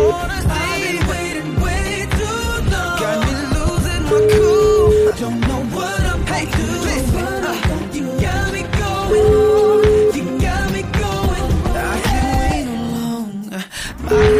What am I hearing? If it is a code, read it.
Ringback tone (Europe, 425 Hz), with noise.